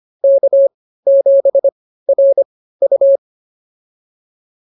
Morse code, heard as a keyed tone, about 25 wpm, spaced as for 15 wpm, 590 Hz. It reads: K7RU
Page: intermediate-practice-files